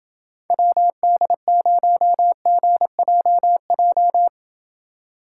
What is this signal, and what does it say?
Morse code, heard as a keyed tone, about 28 wpm, 700 Hz. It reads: WD0GJJ